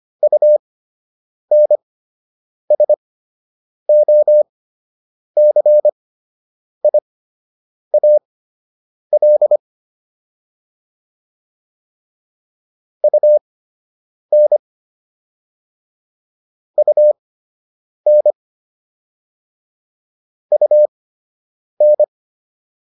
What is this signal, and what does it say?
Morse code, heard as a keyed tone, about 25 wpm, spaced as for 8 wpm, 600 Hz. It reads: UNSOCIAL UN UN UN